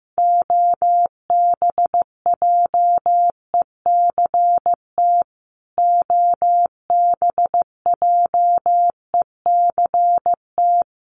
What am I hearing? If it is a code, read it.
Morse code, heard as a keyed tone, about 15 wpm, 700 Hz. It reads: OBJECT OBJECT
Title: Top 500 Words Repeated 15wpm